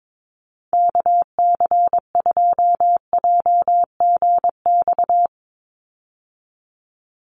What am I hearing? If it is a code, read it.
Morse code, heard as a keyed tone, about 22 wpm, 700 Hz. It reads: KC2JGX
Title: USA Call Signs 22wpm